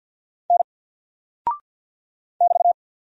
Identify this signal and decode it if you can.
Morse code, heard as a keyed tone, about 48 wpm, 700 Hz. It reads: N <BT>
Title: Single Character (Letter:Number:Punctuation:Prosign) 50wpm